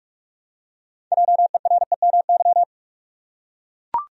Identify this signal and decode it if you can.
Morse code, heard as a keyed tone, about 44 wpm, 700 Hz. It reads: JEREMY
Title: USA Common Names - 45wpm